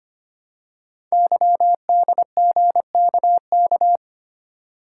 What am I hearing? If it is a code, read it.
Morse code, heard as a keyed tone, about 25 wpm, 700 Hz. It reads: YDGKK
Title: Code Groups - 5 Letters - 25wpm